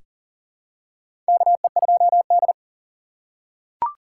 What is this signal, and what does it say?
Morse code, heard as a keyed tone, about 40 wpm, 700 Hz. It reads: KE2D
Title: K1USN SST QSO Practice - 40wpm